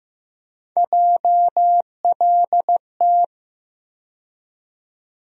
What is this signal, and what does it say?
Morse code, heard as a keyed tone, about 15 wpm, 700 Hz. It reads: JLT
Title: USA Call Sign Postfixes - 3 Letters 15wpm